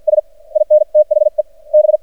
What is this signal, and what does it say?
Morse code, heard as a keyed tone, about 53 wpm, spaced as for 46 wpm, 610 Hz. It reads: S IN THE B
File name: archived / CRSnet